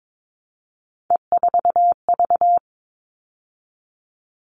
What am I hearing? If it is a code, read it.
Morse code, heard as a keyed tone, about 22 wpm, 700 Hz. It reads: E4V
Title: All Cut Numbers - 3-Digit 22wpm